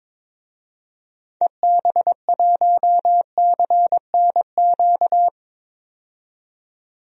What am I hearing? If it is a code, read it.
Morse code, heard as a keyed tone, about 22 wpm, 700 Hz. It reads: EB1CNQ
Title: DXCC-100 Call Signs - 6 Characters - 22wpm